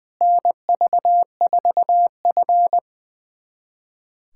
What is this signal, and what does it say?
Morse code, heard as a keyed tone, about 20 wpm, 700 Hz. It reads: NV4F
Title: USA Callsign DE Callsign - 4 Characters - 20wpm